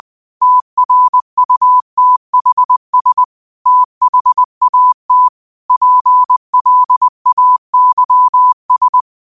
Morse code, heard as a keyed tone, about 20 wpm, 1000 Hz. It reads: TRUTHS THAT PLAYS
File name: qotdfiles / 20wpm